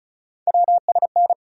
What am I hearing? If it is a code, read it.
Morse code, heard as a keyed tone, about 36 wpm, 700 Hz. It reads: WSN
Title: USA Call Sign Postfixes - 3 Letters 35wpm